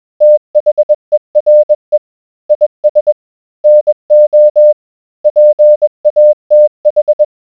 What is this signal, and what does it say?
Morse code, heard as a keyed tone, about 20 wpm, 600 Hz. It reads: THERE IS NO PATH